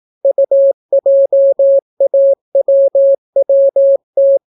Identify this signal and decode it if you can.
Morse code, heard as a keyed tone, about 18 wpm, 550 Hz. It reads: UJAWWT